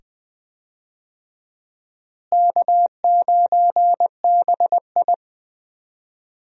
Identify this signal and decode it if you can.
Morse code, heard as a keyed tone, about 21 wpm, 700 Hz. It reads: K9BI
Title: USA Call Signs - 4 Characters - Repeated 2x - 20wpm